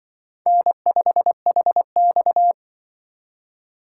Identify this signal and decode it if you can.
Morse code, heard as a keyed tone, about 25 wpm, 700 Hz. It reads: N5HX